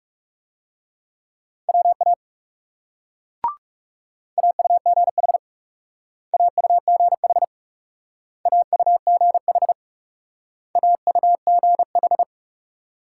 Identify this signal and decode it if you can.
Morse code, heard as a keyed tone, about 36 wpm, 700 Hz. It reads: WA AUGH AUGH AUGH AUGH
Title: Speed-Racing Multi-letter Phonemes - 30wpm